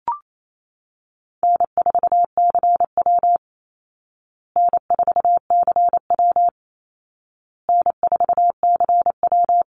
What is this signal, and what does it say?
Morse code, heard as a keyed tone, about 28 wpm, 700 Hz. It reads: N4CW N4CW N4CW
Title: Easy Call Signs Repeated 3x - 28wpm